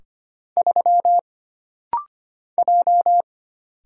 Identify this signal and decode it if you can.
Morse code, heard as a keyed tone, about 25 wpm, 700 Hz. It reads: 3 J
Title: Single Letter-Number Alt - Rapid-Fire 25wpm